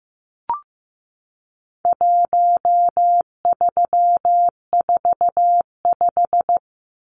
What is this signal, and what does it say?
Morse code, heard as a keyed tone, about 15 wpm, 700 Hz. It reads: 1345
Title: Numbers - 4-Digit 15wpm